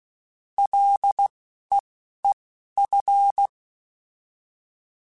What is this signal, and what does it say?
Morse code, heard as a keyed tone, about 16 wpm, spaced as for 12 wpm, 800 Hz. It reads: LEEF
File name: Ls / Single L m3s / Single L Prefixe 3x2 Twice @10